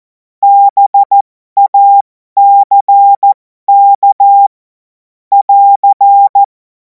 Morse code, heard as a keyed tone, about 14 wpm, spaced as for 12 wpm, 800 Hz. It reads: BACK <AR>
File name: Morsekurs / Witze